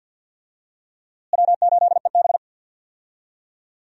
Morse code, heard as a keyed tone, about 49 wpm, 700 Hz. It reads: W8EB
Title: USA Call Signs - 4 Characters - Repeated 3x - 50wpm